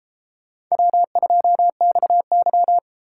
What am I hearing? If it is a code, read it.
Morse code, heard as a keyed tone, about 32 wpm, 700 Hz. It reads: W2XY